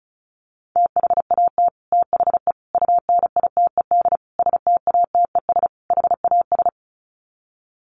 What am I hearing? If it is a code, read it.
Morse code, heard as a keyed tone, about 34 wpm, 700 Hz. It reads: THAT THE UNITED STATES HAS
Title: Sets of 5 Words Encore 35wpm